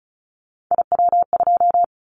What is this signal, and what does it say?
Morse code, heard as a keyed tone, about 36 wpm, 700 Hz. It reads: IW2
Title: Callsign Prefixes for Top 50 DXCC - 35wpm